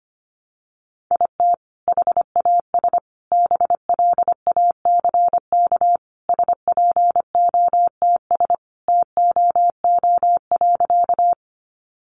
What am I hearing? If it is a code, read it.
Morse code, heard as a keyed tone, about 25 wpm, 700 Hz. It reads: IT HAS BLACK SPOTS TOO.
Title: Additional Easy Phrases 25wpm